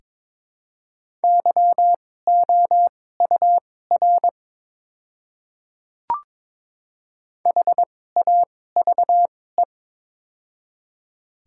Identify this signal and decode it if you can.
Morse code, heard as a keyed tone, about 22 wpm, spaced as for 16 wpm, 700 Hz. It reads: YOUR HAVE
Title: 2x Character Spacing - Top 200 Words - 22wpm